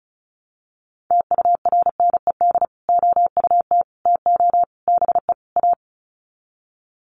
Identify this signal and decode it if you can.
Morse code, heard as a keyed tone, about 34 wpm, 700 Hz. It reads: TURNED OUT TO BE A
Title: Sets of 5 Words 35wpm